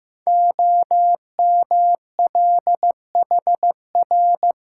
Morse code, heard as a keyed tone, about 15 wpm, 700 Hz. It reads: OMLHR
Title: Code Groups - 5 Letters - 15wpm